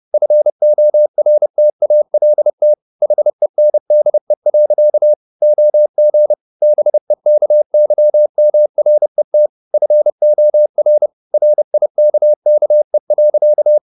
Morse code, heard as a keyed tone, about 30 wpm, 600 Hz. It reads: FORTALT HENDE. OG BEKYMRET FOR RIKKE.
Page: Ferie 30 wpm